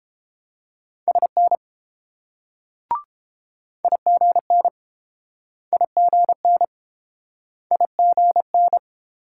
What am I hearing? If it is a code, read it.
Morse code, heard as a keyed tone, about 31 wpm, 690 Hz. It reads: SN IGN IGN IGN